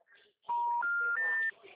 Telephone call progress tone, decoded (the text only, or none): special information tone